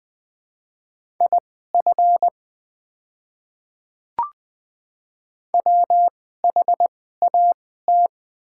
Morse code, heard as a keyed tone, about 20 wpm, spaced as for 14 wpm, 690 Hz. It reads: IF WHAT
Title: 2x Character Spacing - Top 100 Words - 20wpm